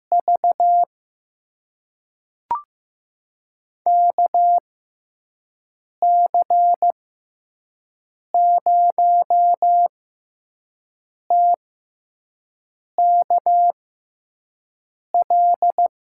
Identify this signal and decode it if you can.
Morse code, heard as a keyed tone, about 15 wpm, spaced as for 5 wpm, 700 Hz. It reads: V KC0TKL
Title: USA Call Signs - 6 Characters - 6x Character Spacing - 15wpm